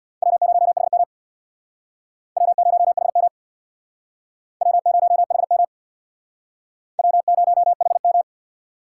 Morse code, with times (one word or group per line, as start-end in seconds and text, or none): 0.2-1.0 W0HM
2.4-3.3 W0HM
4.6-5.6 W0HM
7.0-8.2 W0HM